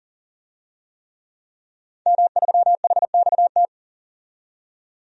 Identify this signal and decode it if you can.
Morse code, heard as a keyed tone, about 40 wpm, 700 Hz. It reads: M3HXT
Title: DXCC-100 Call Signs - 5 Characters - 40wpm